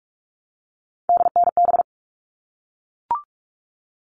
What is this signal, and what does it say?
Morse code, heard as a keyed tone, about 44 wpm, 700 Hz. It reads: DNB
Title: All Cut Numbers - 3-Digit 45wpm